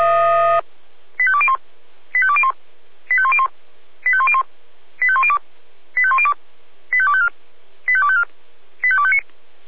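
Five-tone selective call (ZVEI), heard as quick runs of five tones, tones of about 70 ms. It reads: 85201 85201 85101 85101 85102 85102 85254 85254 85259
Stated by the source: ZVEI five-tone call 85850 received by SDR.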